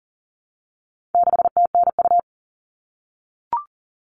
Morse code, heard as a keyed tone, about 40 wpm, 700 Hz. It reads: 6TNU
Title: All Cut Numbers - 4-Digit 40wpm